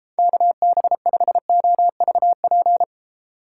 Morse code, heard as a keyed tone, about 34 wpm, 700 Hz. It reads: KB5OVP